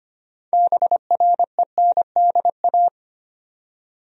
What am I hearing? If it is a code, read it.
Morse code, heard as a keyed tone, about 25 wpm, 700 Hz. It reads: BRENDA
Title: USA Common Names - 25wpm